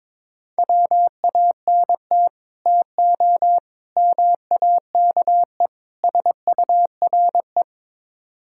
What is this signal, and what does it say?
Morse code, heard as a keyed tone, about 22 wpm, 700 Hz. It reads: WANT TO MAKE SURE
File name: practice / 22wpm / Build A Phrase - 5-Word Phrases - 22wpm